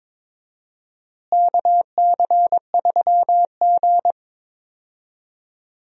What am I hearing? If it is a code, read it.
Morse code, heard as a keyed tone, about 22 wpm, 700 Hz. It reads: KC3G
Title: USA Call Signs - 4 Characters - Repeated 3x - 22wpm